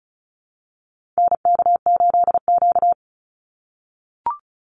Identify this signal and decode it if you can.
Morse code, heard as a keyed tone, about 34 wpm, 700 Hz. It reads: NK8Q
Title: NA SOTA Activator Callsigns 35wpm